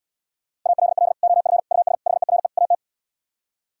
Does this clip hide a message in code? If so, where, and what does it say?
Yes: Morse code from 0.7 to 2.7 s, heard as a keyed tone, about 78 wpm, 700 Hz. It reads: ALL OF US HERE AT